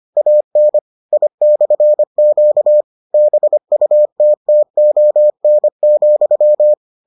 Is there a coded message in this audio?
Yes: Morse code from 0.2 to 6.7 s, heard as a keyed tone, about 25 wpm, 600 Hz. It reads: AN I/Q BUTTON,